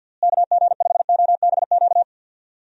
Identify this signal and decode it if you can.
Morse code, heard as a keyed tone, about 49 wpm, 700 Hz. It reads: KG5OBQ
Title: USA Call Signs - 6 Characters - Repeated 2x - 50wpm